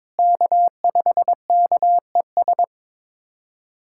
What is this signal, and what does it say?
Morse code, heard as a keyed tone, about 21 wpm, 700 Hz. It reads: K5KES
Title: USA Callsign DE Callsign - 5 Characters - 22wpm